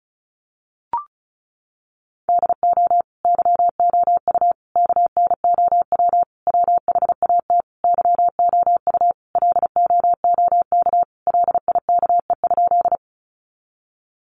Morse code, heard as a keyed tone, about 34 wpm, 700 Hz. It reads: DO YOU KNOW WHAT YOU LOOK LIKE?
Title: Sentences from Top 100 Words 35wpm